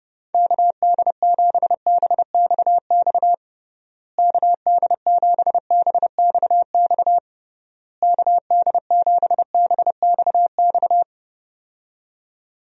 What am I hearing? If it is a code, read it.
Morse code, heard as a keyed tone, about 32 wpm, 700 Hz. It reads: KD7BXX KD7BXX KD7BXX
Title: USA Call Signs - 6 Characters - Repeated 3x - 30wpm